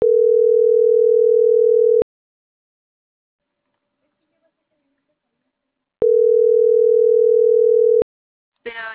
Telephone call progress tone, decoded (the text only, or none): ringback tone